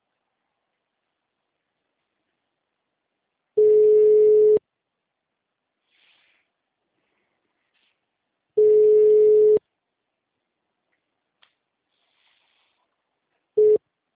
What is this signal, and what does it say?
Ringback tone (Europe, 425 Hz).